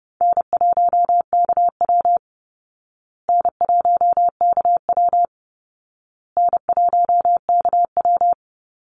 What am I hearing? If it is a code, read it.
Morse code, heard as a keyed tone, about 30 wpm, 700 Hz. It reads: N1KW N1KW N1KW
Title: Easy Call Signs Repeated 3x - 30wpm